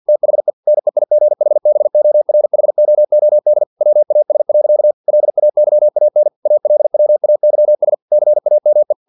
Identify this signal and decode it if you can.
Morse code, heard as a keyed tone, about 50 wpm, 600 Hz. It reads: THE NEIGHBORHOOD WAS. LAYAN ALWAYS CAME
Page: Never 49wpm